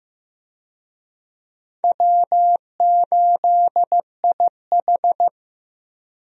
Morse code, heard as a keyed tone, about 15 wpm, 700 Hz. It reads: W8IH
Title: USA Call Signs - 4 Characters - Repeated 2x - 15wpm